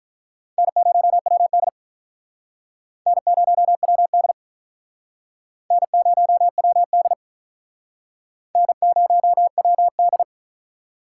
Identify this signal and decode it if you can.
Morse code, heard as a keyed tone, about 46 wpm, 700 Hz. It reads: N0WD N0WD N0WD N0WD